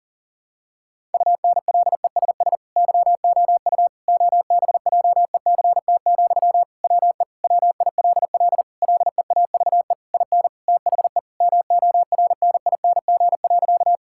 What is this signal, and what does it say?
Morse code, heard as a keyed tone, about 40 wpm, 700 Hz. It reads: UNLESS YOU OBJECT, WE WILL LEAVE IN THE MORNING.
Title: Sentences from Top 1000 Words - Review Part 6 - 40wpm